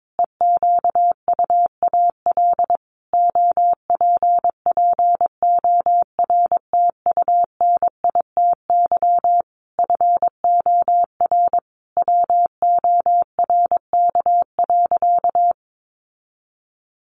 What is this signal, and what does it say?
Morse code, heard as a keyed tone, about 22 wpm, 700 Hz. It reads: EQUAL OPPORTUNITY FOR WORK.
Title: Sentences from Top 1400 Words 22wpm